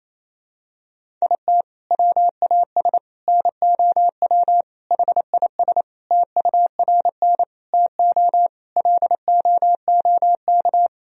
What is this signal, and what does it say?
Morse code, heard as a keyed tone, about 28 wpm, 700 Hz. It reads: IT WAS NOW HIS TURN TO LOOK
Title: Sentences from Top 200 Words and Abbreviations 28wpm